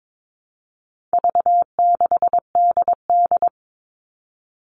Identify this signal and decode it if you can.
Morse code, heard as a keyed tone, about 22 wpm, 700 Hz. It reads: V6DD
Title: All Cut Numbers - 4-Digit 22wpm